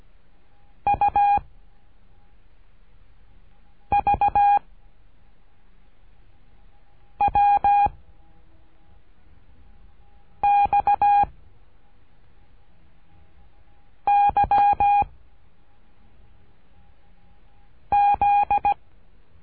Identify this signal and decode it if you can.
Morse code, heard as a keyed tone, about 17 wpm, 810 Hz. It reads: U V W X Y Z
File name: cursos / CURSO DE TELEGRAFIA / Telegrafia - Trilha 13